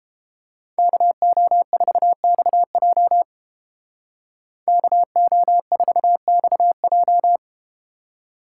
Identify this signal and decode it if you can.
Morse code, heard as a keyed tone, about 32 wpm, 700 Hz. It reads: KO4XJ KO4XJ